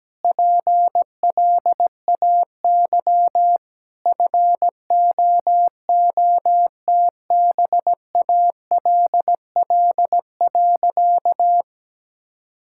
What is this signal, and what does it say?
Morse code, heard as a keyed tone, about 17 wpm, 700 Hz. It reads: PLAY FOOTBALL.
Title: Sentences from Top 1600 Words 17wpm